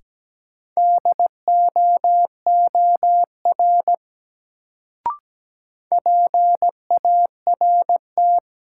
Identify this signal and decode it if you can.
Morse code, heard as a keyed tone, about 17 wpm, 700 Hz. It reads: DOOR PART
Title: Sets of 2 Words - New Word Drill - 17wpm